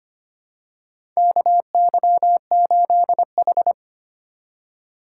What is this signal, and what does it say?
Morse code, heard as a keyed tone, about 25 wpm, 700 Hz. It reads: KY8H
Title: USA Call Signs - 4 Characters - 4x Character Spacing - 25wpm